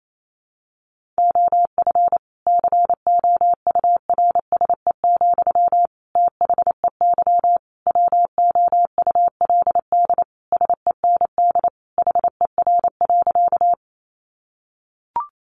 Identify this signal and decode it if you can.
Morse code, heard as a keyed tone, about 28 wpm, 700 Hz. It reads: OF COURSE, THEY WOULD SEND HER.
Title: Sentences from Top 1500 Words 28wpm